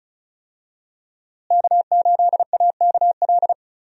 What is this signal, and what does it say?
Morse code, heard as a keyed tone, about 34 wpm, 700 Hz. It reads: K8AKL
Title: USA Call Signs - 5 Characters - Repeated 2x - 35wpm